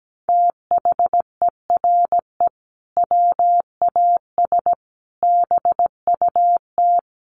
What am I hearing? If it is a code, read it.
Morse code, heard as a keyed tone, about 17 wpm, 700 Hz. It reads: THERE WAS BUT
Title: Sentences from Top 1400 Words 17wpm